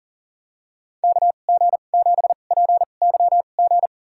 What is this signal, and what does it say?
Morse code, heard as a keyed tone, about 40 wpm, spaced as for 29 wpm, 700 Hz. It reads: KG7PYG
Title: USA Call Signs - 6 Characters - 2x Character Spacing - 40wpm